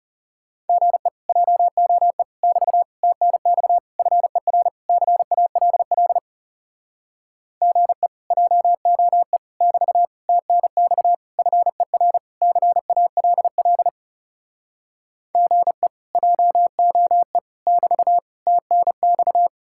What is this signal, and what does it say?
Morse code, heard as a keyed tone, about 40 wpm, 700 Hz. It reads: GE JOE <BT> TNX FER CALL GE JOE <BT> TNX FER CALL GE JOE <BT> TNX